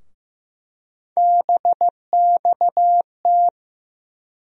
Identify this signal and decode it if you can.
Morse code, heard as a keyed tone, about 15 wpm, 700 Hz. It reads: BXT
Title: USA Call Sign Postfixes - 3 Letters 15wpm